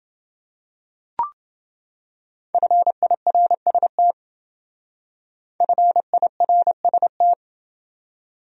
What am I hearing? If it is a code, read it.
Morse code, heard as a keyed tone, about 28 wpm, 700 Hz. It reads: FIRST FIRST